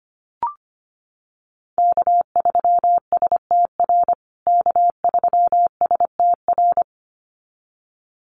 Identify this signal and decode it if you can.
Morse code, heard as a keyed tone, about 25 wpm, 700 Hz. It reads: K3STR K3STR
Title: USA Call Signs Repeated 2x 25wpm